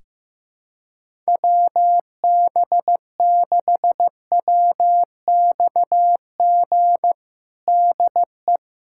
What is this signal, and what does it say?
Morse code, heard as a keyed tone, about 15 wpm, 700 Hz. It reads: WB6WXG DE
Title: USA Callsign DE Callsign 15wpm